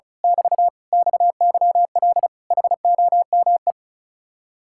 Morse code, heard as a keyed tone, about 32 wpm, 700 Hz. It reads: <BT> XYL HOME
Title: Speed-Racing Common QSO Phrases Encore - 35wpm